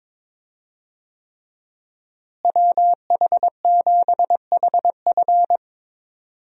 Morse code, heard as a keyed tone, about 22 wpm, 700 Hz. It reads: WH7HF
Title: USA Call Signs - 5 Characters - Repeated 3x - 22wpm